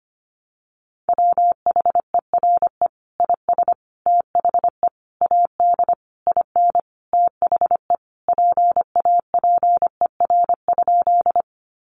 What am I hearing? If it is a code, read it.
Morse code, heard as a keyed tone, about 25 wpm, 700 Hz. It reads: WHERE IS THE AD IN THE PAPER?
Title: Sentences from Top 1300 Words 25wpm